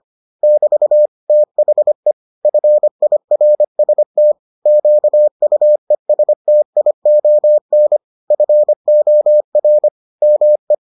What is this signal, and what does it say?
Morse code, heard as a keyed tone, about 25 wpm, 600 Hz. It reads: <BT> THE FIRST QUESTION FOR ME